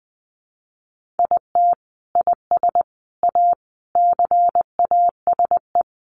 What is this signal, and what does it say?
Morse code, heard as a keyed tone, about 20 wpm, 700 Hz. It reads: IT IS A CASE
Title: Sentences from Top 1400 Words 20wpm